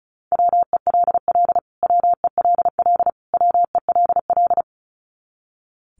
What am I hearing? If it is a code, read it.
Morse code, heard as a keyed tone, about 34 wpm, 700 Hz. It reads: WELL WELL WELL
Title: Top 300 Words Repeated 35wpm